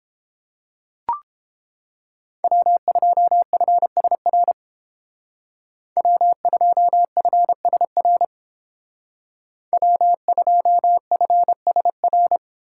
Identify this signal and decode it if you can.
Morse code, heard as a keyed tone, about 28 wpm, 700 Hz. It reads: W2FSR W2FSR W2FSR